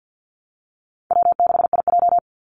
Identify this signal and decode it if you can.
Morse code, heard as a keyed tone, about 50 wpm, 700 Hz. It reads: W6IJ
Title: Easy Call Signs - 50wpm